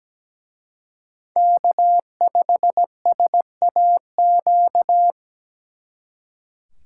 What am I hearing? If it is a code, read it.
Morse code, heard as a keyed tone, about 20 wpm, 700 Hz. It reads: K5SAQ
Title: Speed Racing - USA 1x3 Call Signs 17wpm